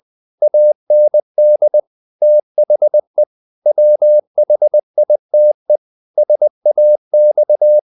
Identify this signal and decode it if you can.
Morse code, heard as a keyed tone, about 20 wpm, 600 Hz. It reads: AND THE WHITE SAX
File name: the old man and the sea / the old man and the sea part 1 20wpm